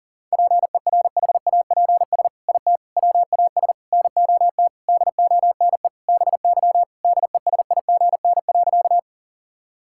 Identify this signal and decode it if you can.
Morse code, heard as a keyed tone, about 37 wpm, 700 Hz. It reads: PERHAPS IT WAS NOT DONE BY DESIGN.
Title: Sentences from Top 1000 Words - Review Part 5 - 40wpm